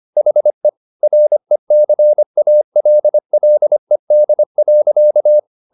Morse code, heard as a keyed tone, about 25 wpm, 600 Hz. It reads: HE RECALLED.